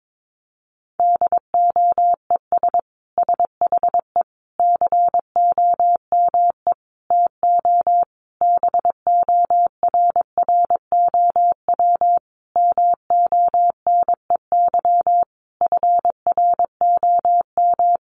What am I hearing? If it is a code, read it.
Morse code, heard as a keyed tone, about 22 wpm, 700 Hz. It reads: DOES SHE COME TO BORROW MONEY FROM
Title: Sentences from Top 1500 Words 22wpm